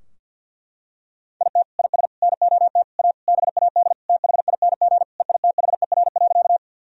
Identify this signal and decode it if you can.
Morse code, heard as a keyed tone, about 50 wpm, 700 Hz. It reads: IT IS NOT A BAD THING EITHER.
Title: Sentences from Top 1000 Words - Review Part 4 - 50wpm